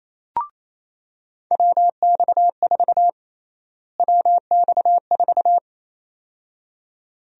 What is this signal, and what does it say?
Morse code, heard as a keyed tone, about 28 wpm, 700 Hz. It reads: WX4 WX4
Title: USA Call Sign Prefixes - Repeated 2x - 28wpm